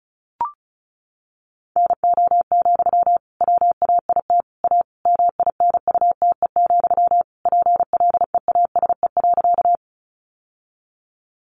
Morse code, heard as a keyed tone, about 34 wpm, 700 Hz. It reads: NO, WAIT A MINUTE, PLEASE.
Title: Sentences from Top 1500 Words 35wpm